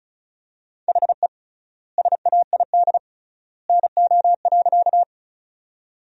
Morse code, heard as a keyed tone, about 36 wpm, 700 Hz. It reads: HE SAID NO.
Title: 3x Word-Spacing - Sentences from Top 100 Words - 35wpm